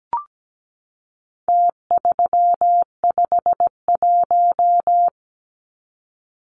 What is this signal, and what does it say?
Morse code, heard as a keyed tone, about 17 wpm, 700 Hz. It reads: T351
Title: Numbers - Contest Style - 4-Digit 17wpm